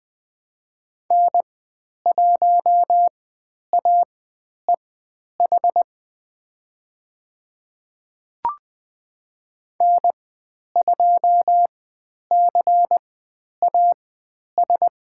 Morse code, heard as a keyed tone, about 20 wpm, spaced as for 10 wpm, 700 Hz. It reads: N1AEH N2CAS
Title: 109 Lesson 18g - Letter m - Call Signs 20wpm